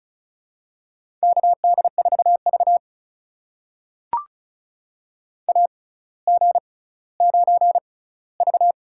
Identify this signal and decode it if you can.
Morse code, heard as a keyed tone, about 36 wpm, spaced as for 18 wpm, 700 Hz. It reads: KD4V AG9V
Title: USA Call Signs - 4 Characters - 6x Character Spacing - 35wpm